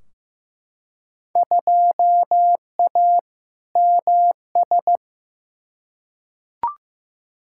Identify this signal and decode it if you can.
Morse code, heard as a keyed tone, about 15 wpm, 700 Hz. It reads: 2A MS
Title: Field Day Exchanges - 15wpm